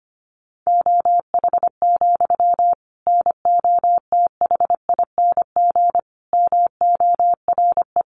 Morse code, heard as a keyed tone, about 25 wpm, 700 Hz. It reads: OH, NOTHING MORE
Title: Sentences from Top 700 Words 25wpm